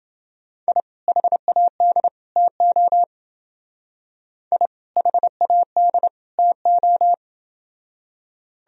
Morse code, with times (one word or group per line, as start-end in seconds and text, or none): 0.7-0.8 I
1.1-2.1 HAD
2.4-3.0 TO
4.5-4.7 I
5.0-6.1 HAD
6.4-7.1 TO